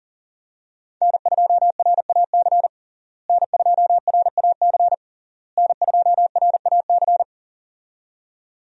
Morse code, heard as a keyed tone, about 37 wpm, 700 Hz. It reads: N2RAC N2RAC N2RAC
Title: USA Call Signs - 5 Characters - Repeated 3x - 40wpm